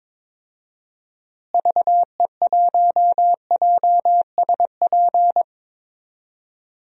Morse code, heard as a keyed tone, about 22 wpm, 700 Hz. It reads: VE1JSP
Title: DXCC-100 Call Signs - 6 Characters - Repeated 2x - 22wpm